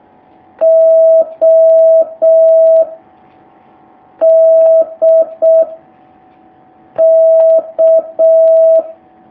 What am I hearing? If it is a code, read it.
Morse code, heard as a keyed tone, about 6 wpm, 650 Hz. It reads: O D K
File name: jh8cbh / 2019cw